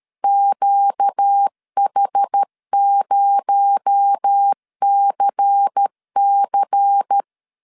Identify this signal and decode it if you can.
Morse code, heard as a keyed tone, about 13 wpm, 800 Hz. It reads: QH0CC
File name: CW / DET / 1 6 1500C S B 12wpm 3431